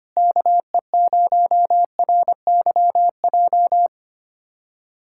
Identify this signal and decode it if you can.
Morse code, heard as a keyed tone, about 25 wpm, 700 Hz. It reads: KE0RYJ
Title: USA Callsign DE Callsign - 6 Characters - 25wpm